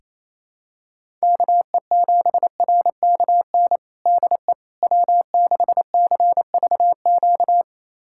Morse code, heard as a keyed tone, about 28 wpm, 700 Hz. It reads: KE7RKN DE W6CVQ
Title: USA Callsign DE Callsign 28wpm